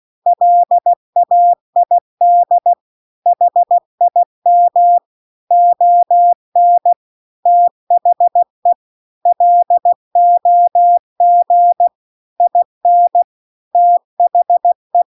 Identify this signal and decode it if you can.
Morse code, heard as a keyed tone, about 16 wpm, 700 Hz. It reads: LAID HIM ON THE LOG IN THE
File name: CW-The-Big-Twoheaded-River-2-16WPM